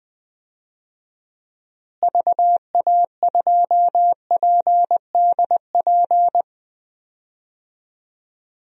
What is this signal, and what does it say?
Morse code, heard as a keyed tone, about 20 wpm, 700 Hz. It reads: VA2PDP
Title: DXCC-100 Call Signs - 6 Characters - Repeated 2x - 20wpm